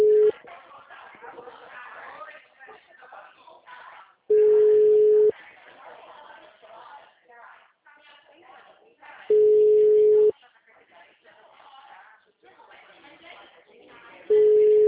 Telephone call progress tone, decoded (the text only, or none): ringback tone